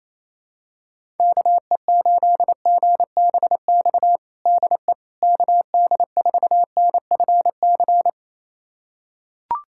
Morse code, heard as a keyed tone, about 27 wpm, 700 Hz. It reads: KE8GBX DE KD4NFC
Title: USA Callsign DE Callsign - 6 Characters - 28wpm